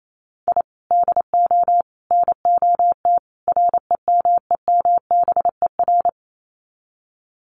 Morse code, heard as a keyed tone, about 28 wpm, 700 Hz. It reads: I DO NOT REMEMBER
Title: Sets of 4 Words 28wpm